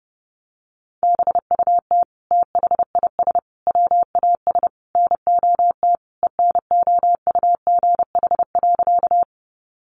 Morse code, heard as a keyed tone, about 30 wpm, 700 Hz. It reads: BUT THIS WAS NOT ENOUGH.
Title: Sentences from Top 600 Words 30wpm